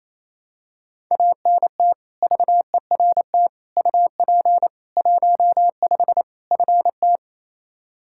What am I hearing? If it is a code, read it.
Morse code, heard as a keyed tone, about 28 wpm, 700 Hz. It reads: ANT VERT UP 15 FT
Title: K1USN SST QSO Practice 28wpm